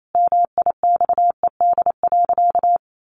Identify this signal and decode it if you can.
Morse code, heard as a keyed tone, about 28 wpm, 700 Hz. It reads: MIXED.
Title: Sentences from Top 1500 Words 28wpm